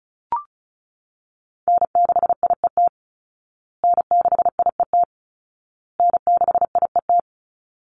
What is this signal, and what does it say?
Morse code, heard as a keyed tone, about 34 wpm, 700 Hz. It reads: N6IET N6IET N6IET
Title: Call Signs Repeated 2x - 35wpm